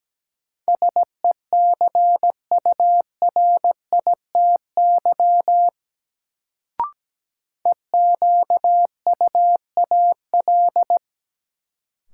Morse code, heard as a keyed tone, about 17 wpm, 700 Hz. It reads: SECURITY EQUAL
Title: Sentences from Top 1400 Words - New Word Drill Alt - 17wpm